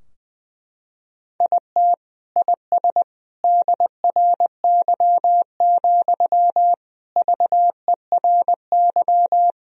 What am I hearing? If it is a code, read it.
Morse code, heard as a keyed tone, about 20 wpm, 700 Hz. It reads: IT IS DRY, VERY
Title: Sentences from Top 1000 Words - Review Part 3 - 20wpm